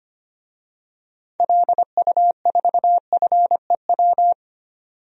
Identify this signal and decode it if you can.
Morse code, heard as a keyed tone, about 25 wpm, 700 Hz. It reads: LU4FEW
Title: DXCC-100 Call Signs - 6 Characters - 25wpm